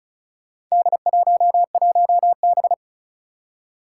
Morse code, heard as a keyed tone, about 36 wpm, 700 Hz. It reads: D11B
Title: Four Challenging Letter-Number Combinations 35wpm